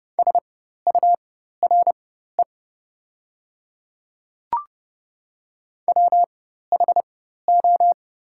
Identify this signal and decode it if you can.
Morse code, heard as a keyed tone, about 30 wpm, spaced as for 14 wpm, 700 Hz. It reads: SURE WHO
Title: 4x Character Spacing - Top 500 Words - 30wpm